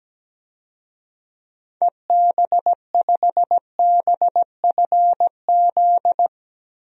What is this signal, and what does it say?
Morse code, heard as a keyed tone, about 17 wpm, 700 Hz. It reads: EB5BFZ
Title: DXCC-100 Call Signs - 6 Characters - Repeated 2x - 17wpm